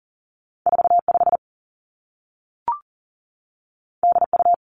Morse code, heard as a keyed tone, about 40 wpm, 700 Hz. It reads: 4 5 D U
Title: Two Challenging Letter-Number Combinations 40wpm